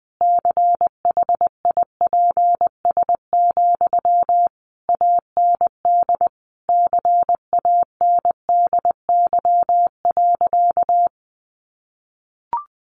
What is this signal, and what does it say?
Morse code, heard as a keyed tone, about 20 wpm, 700 Hz. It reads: CHIPS, AND CANDY.
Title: Easy Phrases 20wpm